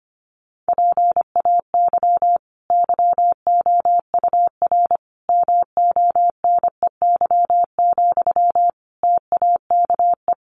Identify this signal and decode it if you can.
Morse code, heard as a keyed tone, about 25 wpm, 700 Hz. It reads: PAY YOUR MONEY, TAKE
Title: Sentences from Top 900 Words 25wpm